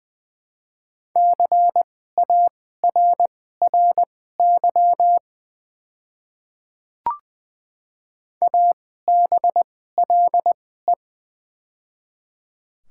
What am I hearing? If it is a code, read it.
Morse code, heard as a keyed tone, about 20 wpm, spaced as for 14 wpm, 690 Hz. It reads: CARRY ABLE
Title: 2x Character Spacing - Top 500 Words - 20wpm